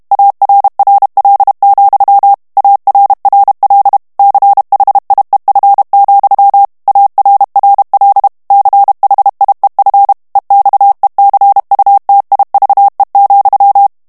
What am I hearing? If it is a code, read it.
Morse code, heard as a keyed tone, about 32 wpm, 800 Hz. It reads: ARRL, ARRL CHIEF, ARRL CHIEF EXECUTIVE,
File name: ANGLAIS texte clair 3 vitesse 30 mots minute